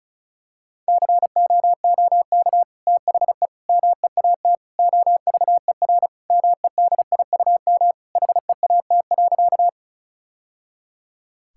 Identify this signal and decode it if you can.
Morse code, heard as a keyed tone, about 36 wpm, 700 Hz. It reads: COOK THE MEAT OVER MEDIUM HEAT.
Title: Sentences from Top 1000 Words - Review Part 6 - 35wpm